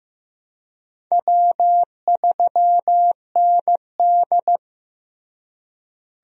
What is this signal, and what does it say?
Morse code, heard as a keyed tone, about 15 wpm, 700 Hz. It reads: W3ND
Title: USA Call Signs - 4 Characters - Repeated 3x - 15wpm